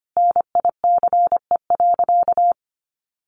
Morse code, heard as a keyed tone, about 25 wpm, 700 Hz. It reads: NICE.
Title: Sentences from Top 1100 Words 25wpm